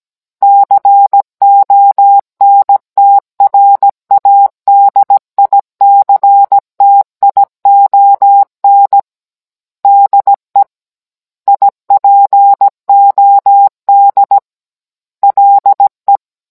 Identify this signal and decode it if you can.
Morse code, heard as a keyed tone, about 17 wpm, 800 Hz. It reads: CONTRADICTION DE IPOD LE